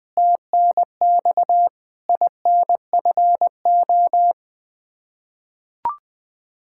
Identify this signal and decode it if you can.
Morse code, heard as a keyed tone, about 20 wpm, 700 Hz. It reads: TNX INFO
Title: Practice QSO - Exchange 3 - More Info - 20wpm